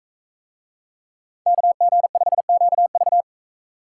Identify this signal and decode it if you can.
Morse code, heard as a keyed tone, about 40 wpm, 700 Hz. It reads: KG5QV